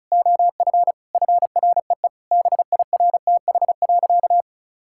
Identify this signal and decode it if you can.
Morse code, heard as a keyed tone, about 34 wpm, 700 Hz. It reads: OF FREE BIRTH.
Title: Sentences from Top 1000 Words - Review Part 5 - 35wpm